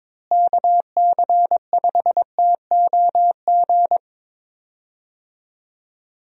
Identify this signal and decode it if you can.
Morse code, heard as a keyed tone, about 22 wpm, 700 Hz. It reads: KC5TOG
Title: USA Call Signs - 6 Characters - Repeated 2x - 22wpm